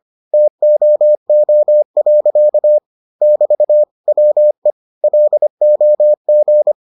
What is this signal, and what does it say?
Morse code, heard as a keyed tone, about 25 wpm, 600 Hz. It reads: TOO. <BT> WE LOG